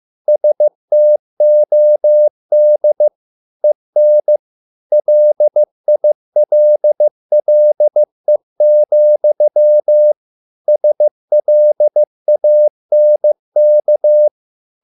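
Morse code, heard as a keyed tone, about 15 wpm, 600 Hz. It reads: STOD EN LILLE, SLANK
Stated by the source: Morse code at a slow 15 wpm from CW med Gnister.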